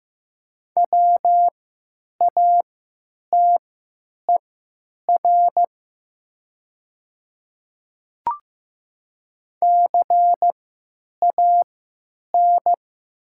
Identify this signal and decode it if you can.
Morse code, heard as a keyed tone, about 15 wpm, spaced as for 9 wpm, 700 Hz. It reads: WATER CAN
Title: 3x Character Spacing - Top 300 Words - 15wpm